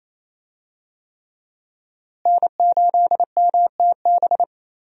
Morse code, heard as a keyed tone, about 28 wpm, 700 Hz. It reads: N8MTB
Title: USA Call Signs - 5 Characters - Repeated 3x - 28wpm